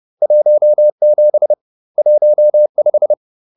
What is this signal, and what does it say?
Morse code, heard as a keyed tone, about 30 wpm, 600 Hz. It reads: 17 15